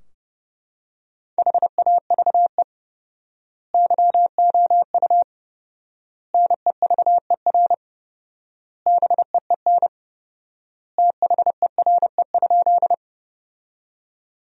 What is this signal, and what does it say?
Morse code, heard as a keyed tone, about 30 wpm, 700 Hz. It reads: HAVE YOU NEVER BEEN THERE?
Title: 4x Word-Spacing - Sentences from Top 300 Words - 30wpm